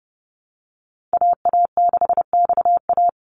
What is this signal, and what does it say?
Morse code, heard as a keyed tone, about 30 wpm, 700 Hz. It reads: AA6XA
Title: NA SOTA Activator Callsigns 30wpm